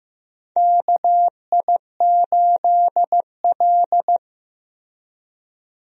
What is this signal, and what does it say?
Morse code, heard as a keyed tone, about 15 wpm, 700 Hz. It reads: KI8L
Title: USA Call Signs - 4 Characters - 15wpm